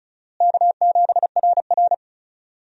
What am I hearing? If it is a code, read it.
Morse code, heard as a keyed tone, about 36 wpm, 700 Hz. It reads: K7RR
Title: USA Call Signs - 4 Characters - Repeated 2x - 35wpm